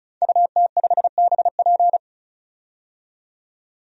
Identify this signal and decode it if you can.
Morse code, heard as a keyed tone, about 36 wpm, 700 Hz. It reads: UT5BP
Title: DXCC-100 Call Signs - 5 Characters - 35wpm